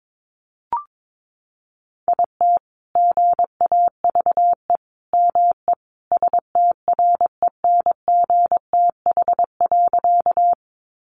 Morse code, heard as a keyed tone, about 22 wpm, 700 Hz. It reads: IT GAVE ME STRENGTH.
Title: Sentences from Top 1300 Words 22wpm